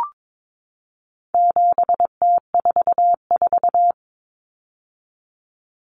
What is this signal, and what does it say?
Morse code, heard as a keyed tone, about 22 wpm, 700 Hz. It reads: 7T44
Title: Common Cut Numbers - 4-Digit 22wpm